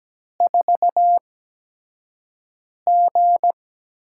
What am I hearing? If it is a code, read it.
Morse code, heard as a keyed tone, about 17 wpm, spaced as for 5 wpm, 700 Hz. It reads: 4G
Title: USA Call Signs - 4 Characters - 8x Character Spacing - 17wpm